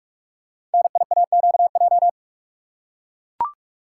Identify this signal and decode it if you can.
Morse code, heard as a keyed tone, about 44 wpm, 700 Hz. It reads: NIAQJ
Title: Code Groups - 5 Letters - 45wpm